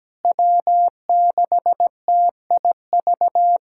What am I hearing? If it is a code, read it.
Morse code, heard as a keyed tone, about 17 wpm, 700 Hz. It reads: W6TIV
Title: USA Callsign DE Callsign 17wpm